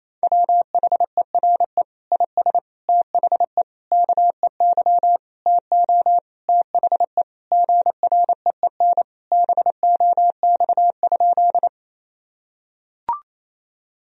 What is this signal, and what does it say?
Morse code, heard as a keyed tone, about 28 wpm, 700 Hz. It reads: WHERE IS THE KEY TO THE GREEN BOX?
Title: Sentences from Top 1000 Words - Review Part 4 - 28wpm